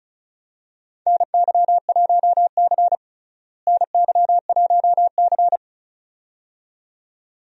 Morse code, heard as a keyed tone, about 34 wpm, 700 Hz. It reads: NY1C NY1C
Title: USA Call Signs - 4 Characters - Repeated 2x - 35wpm